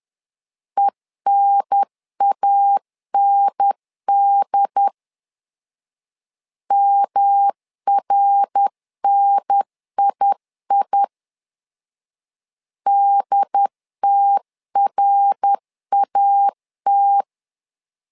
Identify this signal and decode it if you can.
Morse code, heard as a keyed tone, about 10 wpm, 800 Hz. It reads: ENAND MRNII DTRAT
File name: LEZIONI / 3 MWRO / 3 MWR 10wpm 1952